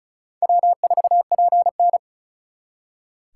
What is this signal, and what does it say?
Morse code, heard as a keyed tone, about 45 wpm, 700 Hz. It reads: W4PN